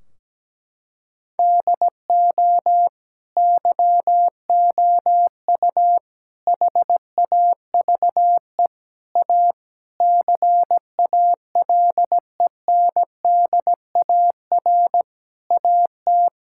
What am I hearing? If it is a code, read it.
Morse code, heard as a keyed tone, about 17 wpm, 700 Hz. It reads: DO YOU HAVE A CALENDAR AT
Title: Sentences from Top 1600 Words 17wpm